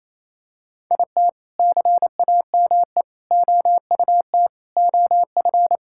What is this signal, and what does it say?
Morse code, heard as a keyed tone, about 28 wpm, 700 Hz. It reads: IT CAME OUT OF
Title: Sentences from Top 1000 Words - Review Part 5 - 28wpm